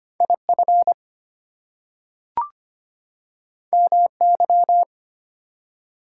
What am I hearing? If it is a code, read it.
Morse code, heard as a keyed tone, about 25 wpm, 700 Hz. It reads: IF MY
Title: Two Letter Words - 25wpm